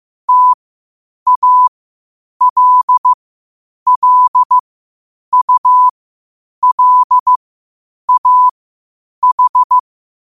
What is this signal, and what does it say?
Morse code, heard as a keyed tone, about 15 wpm, spaced as for 8 wpm, 1000 Hz. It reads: TALLULAH
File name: qotdfiles / 8wpm